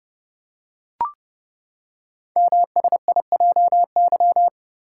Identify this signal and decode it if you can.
Morse code, heard as a keyed tone, about 30 wpm, 700 Hz. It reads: MSIJY
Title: Code Groups - 5 Letters - 30wpm